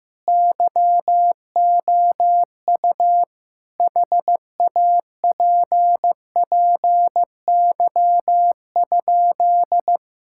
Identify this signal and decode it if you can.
Morse code, heard as a keyed tone, about 15 wpm, 700 Hz. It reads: YOU HAPPY?
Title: Sentences from Top 1000 Words - Review Part 6 - 15wpm